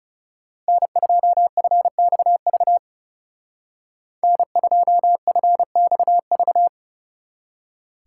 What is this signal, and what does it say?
Morse code, heard as a keyed tone, about 36 wpm, 700 Hz. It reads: N2FXV N2FXV
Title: Speed Racing - USA 1x3 Call Signs 30wpm